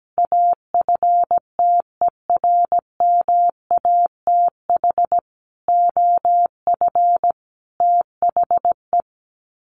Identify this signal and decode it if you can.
Morse code, heard as a keyed tone, about 17 wpm, 700 Hz. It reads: AFTERMATH OF THE
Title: Sets of 5 Words 17wpm